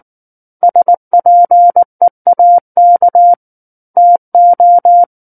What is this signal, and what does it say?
Morse code, heard as a keyed tone, about 19 wpm, 700 Hz. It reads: SPEAK TO
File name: Book One / 10-Chapter 10 - 19 WPM